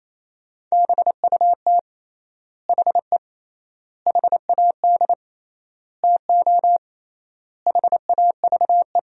Morse code, heard as a keyed tone, about 28 wpm, 700 Hz. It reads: BUT HE HAD TO HAVE
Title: 3x Word-Spacing - Sentences from Top 100 Words - 28wpm